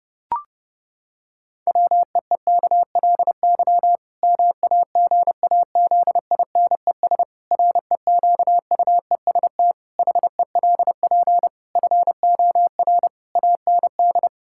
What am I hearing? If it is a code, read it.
Morse code, heard as a keyed tone, about 30 wpm, 700 Hz. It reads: WEEKLY MAGAZINES REQUEST HELP FOR AND
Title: Pangram Challenge v2 - 30wpm